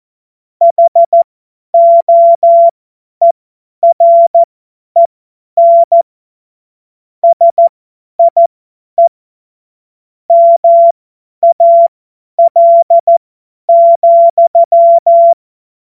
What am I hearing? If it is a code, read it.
Morse code, heard as a keyed tone, about 14 wpm, spaced as for 10 wpm, 680 Hz. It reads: HOEREN SIE MAL,
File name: Morsekurs / Witze / Witz 09. Februar 2021, 50 F70